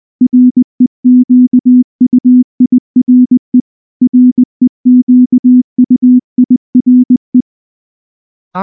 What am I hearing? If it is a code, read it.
Morse code, heard as a keyed tone, about 19 wpm, 250 Hz. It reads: REQUIRE REQUIRE
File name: morse / morse-training-20WPM-2